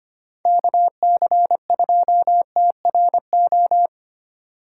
Morse code, heard as a keyed tone, about 25 wpm, 700 Hz. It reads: KC2TRO
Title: USA Call Signs - 6 Characters - 3x Character Spacing - 25wpm